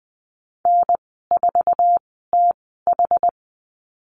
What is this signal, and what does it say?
Morse code, heard as a keyed tone, about 20 wpm, spaced as for 14 wpm, 700 Hz. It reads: N4TH
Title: 062 Lesson 11g - Letter h - Call Signs 20wpm